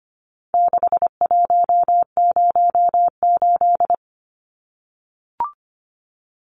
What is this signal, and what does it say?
Morse code, heard as a keyed tone, about 25 wpm, 700 Hz. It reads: 6108
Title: Numbers - 4-Digit 25wpm